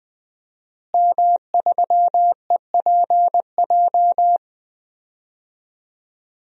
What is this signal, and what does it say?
Morse code, heard as a keyed tone, about 20 wpm, 700 Hz. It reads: M3EPJ
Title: DXCC-100 Call Signs - 5 Characters - Repeated 2x - 20wpm